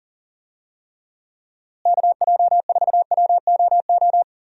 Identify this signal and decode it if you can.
Morse code, heard as a keyed tone, about 40 wpm, 700 Hz. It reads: KJ4WOO
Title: USA Call Signs - 6 Characters - Repeated 3x - 40wpm